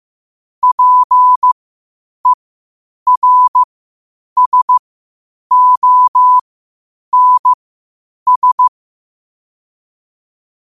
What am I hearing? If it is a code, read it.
Morse code, heard as a keyed tone, about 15 wpm, spaced as for 8 wpm, 1000 Hz. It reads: PERSONS
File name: qotdfiles / 8wpm